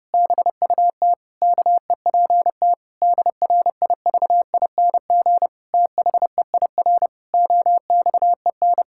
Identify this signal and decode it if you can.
Morse code, heard as a keyed tone, about 32 wpm, 700 Hz. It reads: BUT KEPT DRIVING THEIR OXEN
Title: Pangram Challenge v2 - 30wpm